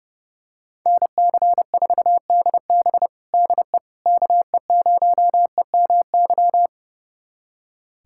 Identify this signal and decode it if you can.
Morse code, heard as a keyed tone, about 30 wpm, 700 Hz. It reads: NC4DB DE KE0EMY
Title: USA Callsign DE Callsign 30wpm